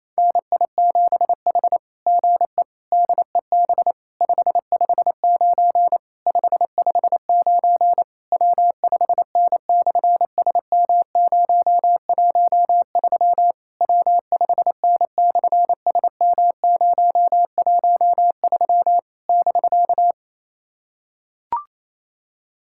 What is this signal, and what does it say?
Morse code, heard as a keyed tone, about 27 wpm, 700 Hz. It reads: NI7H GE DEB 559 559 W5N/SM013 W5N/SM013 <BK>
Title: SOTA QSO Practice 28wpm